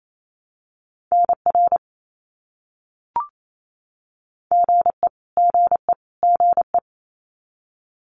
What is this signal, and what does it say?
Morse code, heard as a keyed tone, about 28 wpm, 700 Hz. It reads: NR GE GE GE
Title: Abbreviations Repeated 3x - 28wpm